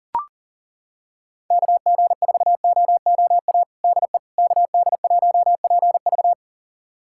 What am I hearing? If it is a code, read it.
Morse code, heard as a keyed tone, about 40 wpm, 700 Hz. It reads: KG4OOA DE KD1PV
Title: USA Callsign DE Callsign 40wpm